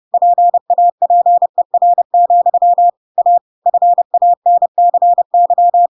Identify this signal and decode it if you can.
Morse code, heard as a keyed tone, about 30 wpm, 700 Hz. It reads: PAPER, A FANCY
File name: Pinocchio / Pinocchio ch8 30 30 ik1hge